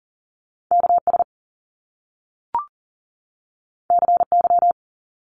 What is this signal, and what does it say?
Morse code, heard as a keyed tone, about 39 wpm, 700 Hz. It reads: KS CY
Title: Top Two Letter Combinations in Words 40wpm